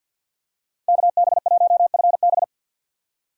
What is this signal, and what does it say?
Morse code, heard as a keyed tone, about 53 wpm, 690 Hz. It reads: KB1FB
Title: USA Call Signs - 5 Characters - 6x Character Spacing - 50wpm